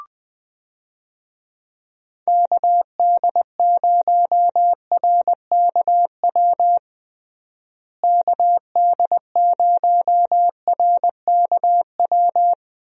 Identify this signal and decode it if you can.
Morse code, heard as a keyed tone, about 20 wpm, 700 Hz. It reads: KD0RKW KD0RKW
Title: USA Call Signs - 6 Characters - Repeated 3x - 20wpm